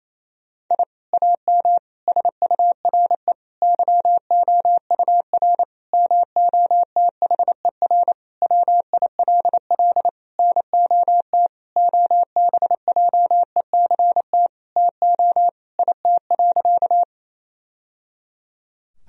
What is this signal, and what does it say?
Morse code, heard as a keyed tone, about 28 wpm, 700 Hz. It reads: I AM SURE YOUR MOTHER WILL NOT OBJECT TO IT.
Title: Sentences from Top 1000 Words - Review Part 3 - 28wpm